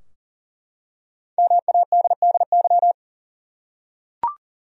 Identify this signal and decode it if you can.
Morse code, heard as a keyed tone, about 40 wpm, 700 Hz. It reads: MADDY
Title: USA Common Names - Max 5 Letters - 40wpm